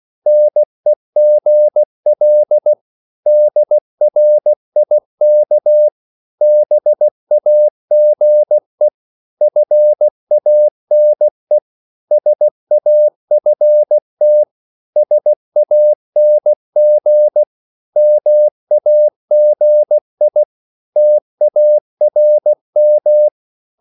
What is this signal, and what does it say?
Morse code, heard as a keyed tone, about 16 wpm, 600 Hz. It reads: NEGL DRIK BAGE FANE SAFT SANG MAGI TARM
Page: Korte danske ord 16wpm